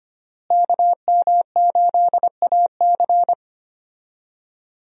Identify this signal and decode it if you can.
Morse code, heard as a keyed tone, about 25 wpm, 700 Hz. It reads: KM8AC
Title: USA Call Signs - 5 Characters - Repeated 3x - 25wpm